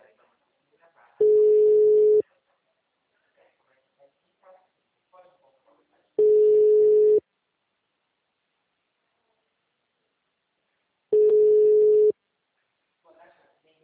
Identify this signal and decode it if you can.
Ringback tone (Europe, 425 Hz), with speech.